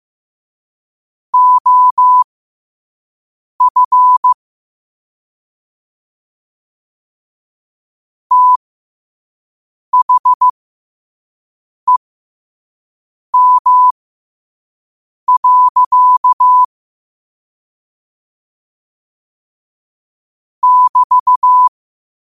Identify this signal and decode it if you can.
Morse code, heard as a keyed tone, about 15 wpm, spaced as for 5 wpm, 1000 Hz. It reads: OF THEM. <BT>
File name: qotdfiles / 5wpm